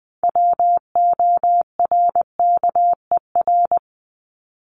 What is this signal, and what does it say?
Morse code, heard as a keyed tone, about 20 wpm, 700 Hz. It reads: WORKER
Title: Top 1751-2000 Words 20wpm